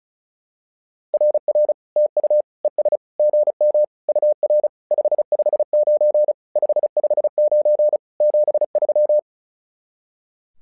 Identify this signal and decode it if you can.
Morse code, heard as a keyed tone, about 36 wpm, 600 Hz. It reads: RR TU ES GM UR 559 559 73
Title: SOTA QSO Practice 35wpm